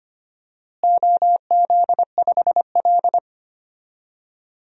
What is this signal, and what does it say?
Morse code, heard as a keyed tone, about 25 wpm, 700 Hz. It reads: OZ5L
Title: DXCC-100 Call Signs - 4 Characters - Repeated 3x - 25wpm